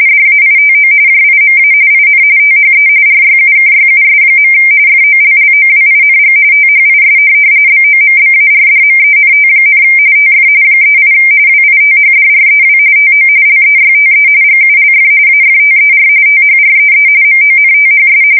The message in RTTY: DAITO JIMA IN THE RYUKYUS. FOLLOWING THIS BOMBARDMENT, TF 38 WITHDREW TO LEYTE GULF AFTER THREE MONTHS

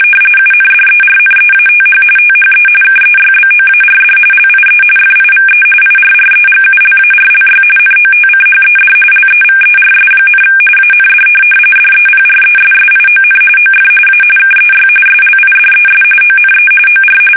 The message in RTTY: UNDERNEATH THE KEYBOARD LIE FIVE IRREGULARLY NOTCHED LEVERS, AND DEPRESSION OF A KEY SLIDES ONE OR

